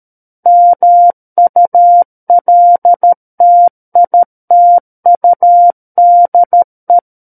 Morse code, heard as a keyed tone, about 13 wpm, 700 Hz. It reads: MULTITUDE
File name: Book One / 04-Chapter 4 - 13 WPM